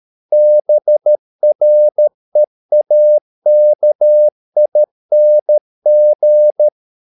Morse code, heard as a keyed tone, about 13 wpm, 600 Hz. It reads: BREAKING